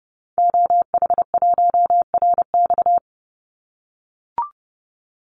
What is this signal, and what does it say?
Morse code, heard as a keyed tone, about 30 wpm, 700 Hz. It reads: OH1RX
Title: Call Signs - 30wpm